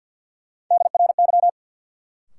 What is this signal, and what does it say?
Morse code, heard as a keyed tone, about 50 wpm, 700 Hz. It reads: DRY
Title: Sentences from Top 500 Words - New Word Drill - 50wpm